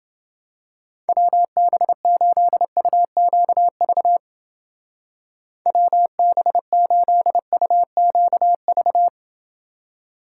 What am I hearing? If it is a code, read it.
Morse code, heard as a keyed tone, about 29 wpm, 700 Hz. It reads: WB8UQV WB8UQV